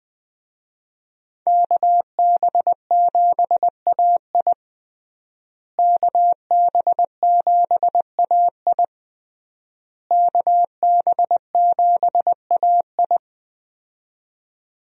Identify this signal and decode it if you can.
Morse code, heard as a keyed tone, about 20 wpm, 700 Hz. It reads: KB7AI KB7AI KB7AI
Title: USA Call Signs - 5 Characters - Repeated 3x - 20wpm